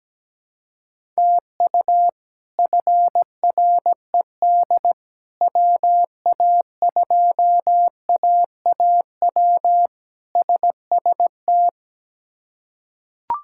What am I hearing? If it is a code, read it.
Morse code, heard as a keyed tone, about 17 wpm, 700 Hz. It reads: TU FRED WA2AAW SST
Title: K1USN SST QSO Practice - 17wpm